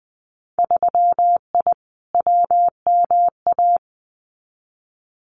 Morse code, heard as a keyed tone, about 20 wpm, 700 Hz. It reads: 3I WMA
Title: Winter Field Day Exchanges 20wpm